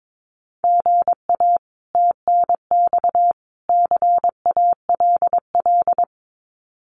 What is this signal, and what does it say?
Morse code, heard as a keyed tone, about 22 wpm, 700 Hz. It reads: GA TNX CALL
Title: Common QSO Phrases 22wpm